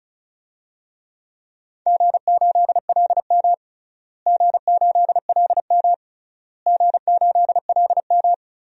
Morse code, heard as a keyed tone, about 36 wpm, 700 Hz. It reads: G8LM G8LM G8LM
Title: DXCC-100 Call Signs - 4 Characters - Repeated 3x - 35wpm